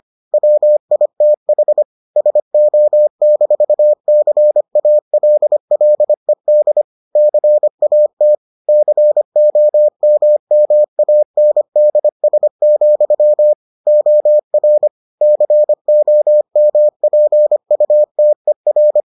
Morse code, heard as a keyed tone, about 25 wpm, 600 Hz. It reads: WITH SO-CALLED CAT COMMANDS, OR COMPUTER